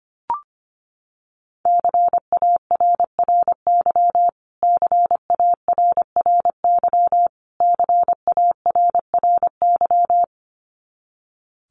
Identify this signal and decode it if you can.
Morse code, heard as a keyed tone, about 25 wpm, 700 Hz. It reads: CARRY CARRY CARRY
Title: Top 500 Words Repeated 25wpm